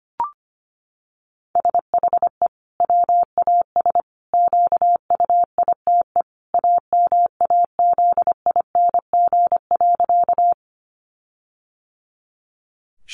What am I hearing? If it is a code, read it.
Morse code, heard as a keyed tone, about 25 wpm, 700 Hz. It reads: SHE WAS QUITE AMAZING.
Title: Sentences from Top 1500 Words 25wpm